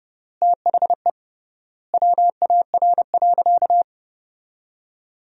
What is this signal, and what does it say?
Morse code, heard as a keyed tone, about 30 wpm, 700 Hz. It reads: THE WAR.
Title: 3x Word-Spacing - Sentences from Top 400 Words - 30wpm